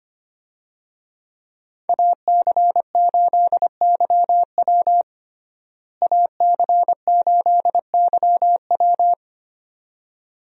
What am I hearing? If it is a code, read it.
Morse code, heard as a keyed tone, about 25 wpm, 700 Hz. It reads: AC8YW AC8YW
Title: USA Call Signs - 5 Characters - Repeated 2x - 25wpm